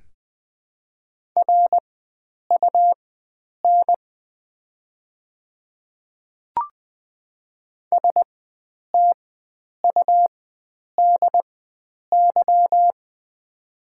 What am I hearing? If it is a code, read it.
Morse code, heard as a keyed tone, about 20 wpm, spaced as for 9 wpm, 700 Hz. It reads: RUN STUDY
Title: 4x Character Spacing - Top 300 Words - 20wpm